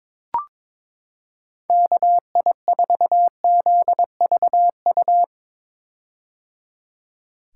Morse code, heard as a keyed tone, about 21 wpm, 700 Hz. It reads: KI4ZVU
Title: K1USN SST QSO Practice - 22wpm